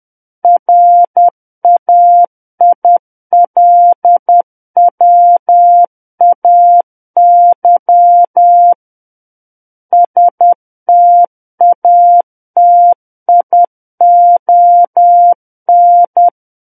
Morse code, heard as a keyed tone, about 10 wpm, 700 Hz. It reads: RAILWAY STATION
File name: Book One / 01-Chapter 1 - 10 WPM